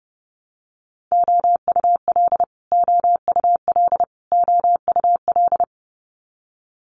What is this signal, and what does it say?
Morse code, heard as a keyed tone, about 30 wpm, 700 Hz. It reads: OUL OUL OUL
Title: Multi-letter Phonemes Repeated 3x 30wpm